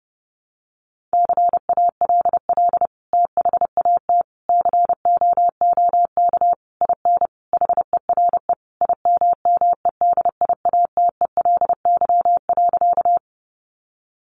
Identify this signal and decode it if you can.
Morse code, heard as a keyed tone, about 30 wpm, 700 Hz. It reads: CALL THAT COOK IN HERE IMMEDIATELY.
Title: Sentences from Top 1100 Words 30wpm